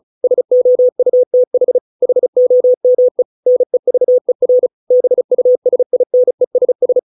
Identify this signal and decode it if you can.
Morse code, heard as a keyed tone, about 34 wpm, 500 Hz. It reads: SOUTH HOME NEVER BUSINESS